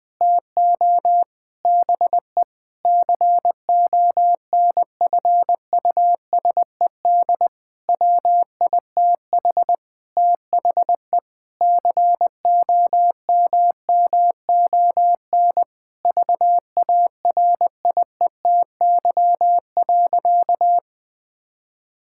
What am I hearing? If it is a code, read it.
Morse code, heard as a keyed tone, about 20 wpm, 700 Hz. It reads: TO BE CONFUSED WITH THE COMMON VARIETY.
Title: Sentences with 6-Word Phrases - Spoken Beginning - 20wpm